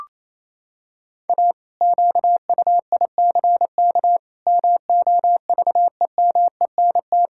Morse code, heard as a keyed tone, about 27 wpm, 700 Hz. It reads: A QUICK MOVEMENT
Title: Pangram Challenge v2 - 28wpm